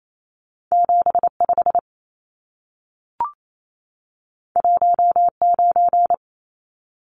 Morse code, heard as a keyed tone, about 28 wpm, 700 Hz. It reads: 75 19
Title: Numbers - 2-Digit 28wpm